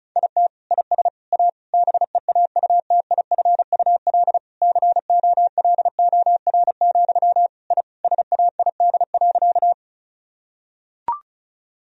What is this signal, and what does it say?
Morse code, heard as a keyed tone, about 36 wpm, 700 Hz. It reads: IT IS A BEAUTIFUL COLOR, I SAID.
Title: Sentences from Top 1000 Words - Review Part 2 - 35wpm